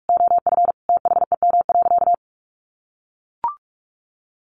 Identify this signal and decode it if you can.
Morse code, heard as a keyed tone, about 44 wpm, 700 Hz. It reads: OF THEM.
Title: Sentences from Top 300 Words 45wpm